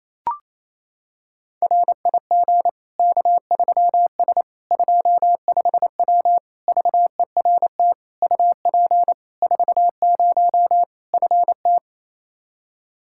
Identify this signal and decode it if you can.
Morse code, heard as a keyed tone, about 28 wpm, 700 Hz. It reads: RIG K3S 25W VERT UP 40 FT
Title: K1USN SST QSO Practice 28wpm